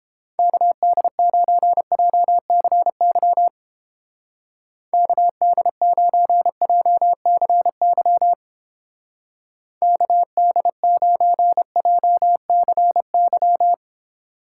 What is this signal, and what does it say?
Morse code, heard as a keyed tone, about 29 wpm, 700 Hz. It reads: KD9JCY KD9JCY KD9JCY